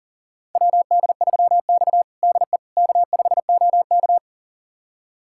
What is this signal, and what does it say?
Morse code, heard as a keyed tone, about 40 wpm, 700 Hz. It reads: WD3X DE K5OK
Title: USA Callsign DE Callsign - 4 Characters - 40wpm